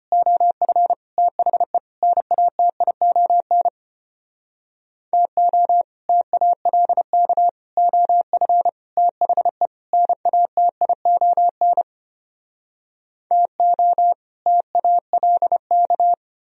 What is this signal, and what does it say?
Morse code, heard as a keyed tone, about 33 wpm, 700 Hz. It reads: OF THE NATION TO TALK OF THE NATION TO TALK